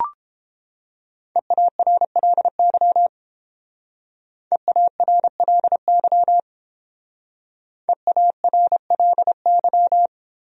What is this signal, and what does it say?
Morse code, heard as a keyed tone, about 30 wpm, 700 Hz. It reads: EARLY EARLY EARLY